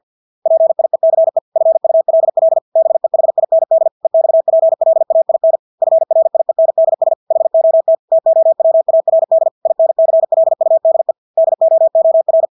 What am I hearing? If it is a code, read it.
Morse code, heard as a keyed tone, about 50 wpm, 650 Hz. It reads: PIECE FULL BEHIND EXPLAIN FRIENDS HOT TOWARD INCLUDE DOOR
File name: Engelske ord 50 wpm